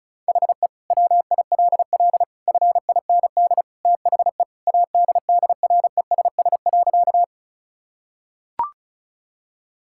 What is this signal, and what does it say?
Morse code, heard as a keyed tone, about 36 wpm, 700 Hz. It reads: HE WILL FIND THE ADDRESS.
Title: Sentences from Top 1000 Words - Review Part 6 - 35wpm